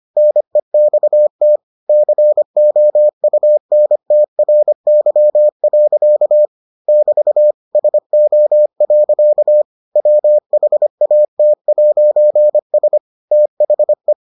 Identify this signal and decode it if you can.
Morse code, heard as a keyed tone, about 25 wpm, 600 Hz. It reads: NEXT COUNTRY. <BT> SO. WHAT'S THE